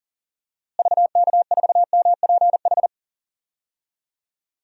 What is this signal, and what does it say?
Morse code, heard as a keyed tone, about 37 wpm, 700 Hz. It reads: VK4MPH
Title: DXCC-100 Call Signs - 6 Characters - Repeated 3x - 40wpm